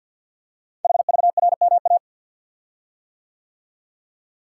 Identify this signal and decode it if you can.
Morse code, heard as a keyed tone, about 48 wpm, 690 Hz. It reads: HVRMA
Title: Code Groups - 5 Letters - 50wpm